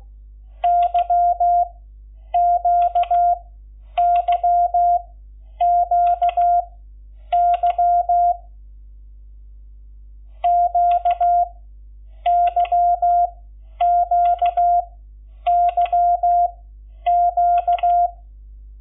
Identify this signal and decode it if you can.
Morse code, heard as a keyed tone, about 16 wpm, spaced as for 9 wpm, 700 Hz. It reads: YQYQY QYQYQ
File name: Lecciones / CLASE Nº11